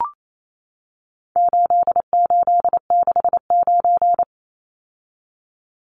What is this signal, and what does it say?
Morse code, heard as a keyed tone, about 28 wpm, 700 Hz. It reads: 8869
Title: Numbers - 4-Digit 28wpm